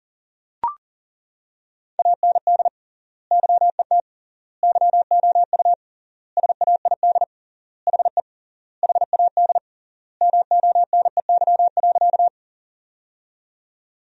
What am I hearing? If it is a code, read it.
Morse code, heard as a keyed tone, about 40 wpm, 700 Hz. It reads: AND YET YOU SAID HE HAD MONEY.
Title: 3x Word-Spacing - Sentences from Top 500 Words - 40wpm